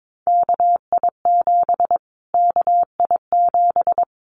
Morse code, heard as a keyed tone, about 22 wpm, 700 Hz. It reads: KI7 KI7
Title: Callsign Prefixes for Top 25 DXCC - Repeated 3x - 22wpm